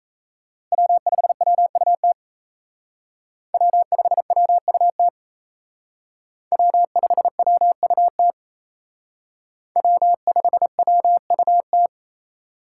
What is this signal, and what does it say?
Morse code, heard as a keyed tone, about 35 wpm, 700 Hz. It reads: W5WUT W5WUT W5WUT W5WUT